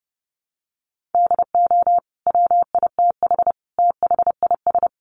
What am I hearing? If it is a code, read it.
Morse code, heard as a keyed tone, about 30 wpm, 700 Hz. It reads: DO WITH THIS
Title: Sentences from Top 1400 Words 30wpm